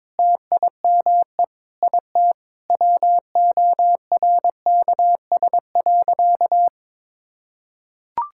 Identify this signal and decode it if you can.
Morse code, heard as a keyed tone, about 22 wpm, 700 Hz. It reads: TIME IT WORKS.
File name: practice / 22wpm / Sentences with 3-Word Phrases - 22wpm